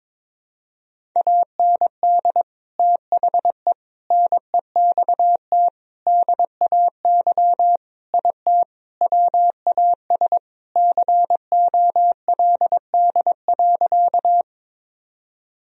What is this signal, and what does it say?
Morse code, heard as a keyed tone, about 22 wpm, 700 Hz. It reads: AND THE NEXT DAY IT WAS COLD.
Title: Sentences from Top 1000 Words - Review Part 1 - 22wpm